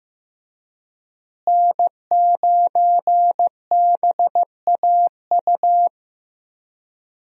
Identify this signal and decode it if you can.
Morse code, heard as a keyed tone, about 15 wpm, 700 Hz. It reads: N9BAU
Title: USA Call Signs - 5 Characters - Repeated 3x - 15wpm